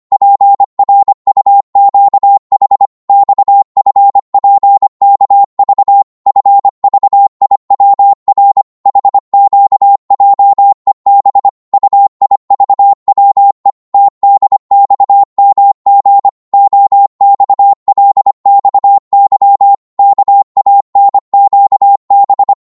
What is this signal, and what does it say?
Morse code, heard as a keyed tone, about 25 wpm, 800 Hz. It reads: PRUQH XFPKV FVIWR HQJEB UIVWE TDXMG OXLXY KANQB
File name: Morse / ltr-25wpm